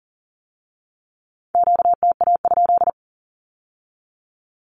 Morse code, heard as a keyed tone, about 40 wpm, 700 Hz. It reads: QTA?
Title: Q-Codes with Spoken Meaning - 40wpm